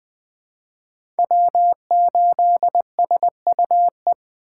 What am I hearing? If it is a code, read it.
Morse code, heard as a keyed tone, about 20 wpm, 700 Hz. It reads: W8SUE